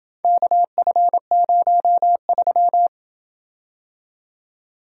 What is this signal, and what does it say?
Morse code, heard as a keyed tone, about 27 wpm, 700 Hz. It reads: KF03